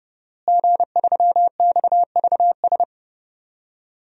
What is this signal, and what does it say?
Morse code, heard as a keyed tone, about 30 wpm, 700 Hz. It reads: G3XVS
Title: DXCC-100 Call Signs - 5 Characters - Repeated 2x - 30wpm